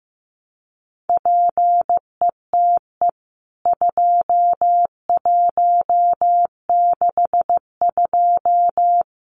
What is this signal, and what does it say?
Morse code, heard as a keyed tone, about 15 wpm, 700 Hz. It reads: PETE 2162
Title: CWT Exchange 15wpm